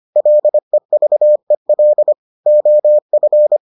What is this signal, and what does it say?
Morse code, heard as a keyed tone, about 25 wpm, 600 Hz. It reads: LEVEL OF